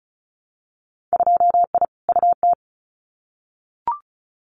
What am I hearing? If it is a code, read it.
Morse code, heard as a keyed tone, about 34 wpm, 700 Hz. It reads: 2I UT
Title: Winter Field Day Exchanges 35wpm